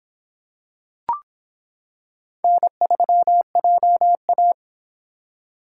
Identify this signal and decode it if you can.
Morse code, heard as a keyed tone, about 25 wpm, 700 Hz. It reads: N3JA